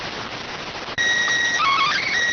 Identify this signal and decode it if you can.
SSTV (slow-scan television), PD 290.